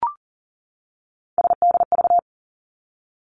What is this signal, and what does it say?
Morse code, heard as a keyed tone, about 38 wpm, 700 Hz. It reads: SDV
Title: Three Challenging Letter-Number Combinations 40wpm